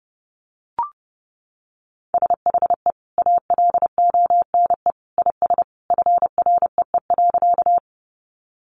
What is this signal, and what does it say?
Morse code, heard as a keyed tone, about 30 wpm, 700 Hz. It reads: SHE ALONE IS FREE.
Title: Sentences from Top 1000 Words 30wpm